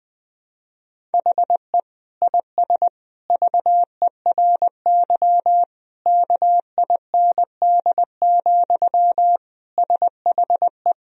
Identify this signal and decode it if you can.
Morse code, heard as a keyed tone, about 20 wpm, 700 Hz. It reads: HE IS VERY KIND, SHE
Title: Sentences from Top 1000 Words - Review Part 1 - 20wpm